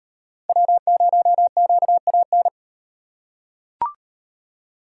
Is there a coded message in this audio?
Yes: Morse code from 0.5 to 2.5 s, heard as a keyed tone, about 34 wpm, 700 Hz. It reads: W0QAN